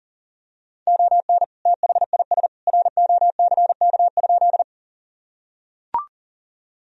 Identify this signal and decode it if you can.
Morse code, heard as a keyed tone, about 40 wpm, 700 Hz. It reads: ON THIS ROCK?
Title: Sentences from Top 1000 Words - Review Part 2 - 40wpm